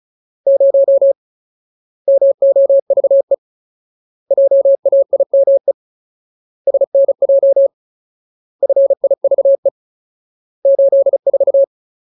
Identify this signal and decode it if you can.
Morse code, heard as a keyed tone, about 34 wpm, 550 Hz. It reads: 0 MOVE JAIME SNJ FIVE 84
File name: Ultimate Word List / 35WPM Ultimate